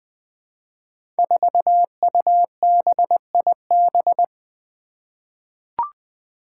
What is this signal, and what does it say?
Morse code, heard as a keyed tone, about 20 wpm, 700 Hz. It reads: 4UBIB
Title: Five Challenging Letter-Number Combinations 20wpm